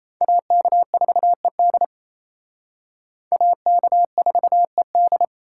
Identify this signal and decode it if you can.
Morse code, heard as a keyed tone, about 34 wpm, 700 Hz. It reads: AK4ED AK4ED
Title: Speed Racing - USA 2x2 Call Signs 28wpm